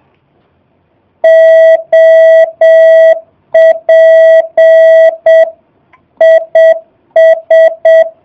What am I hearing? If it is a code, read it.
Morse code, heard as a keyed tone, about 7 wpm, 650 Hz. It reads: OP IS
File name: jh8cbh / 2019cw / SP05- w